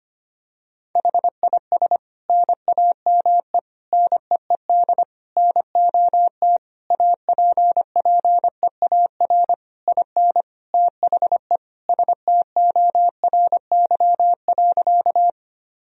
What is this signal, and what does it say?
Morse code, heard as a keyed tone, about 25 wpm, 700 Hz. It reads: HIS NAME NEED NOT APPEAR IN THE STORY.
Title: Sentences from Top 1000 Words - Review Part 2 - 25wpm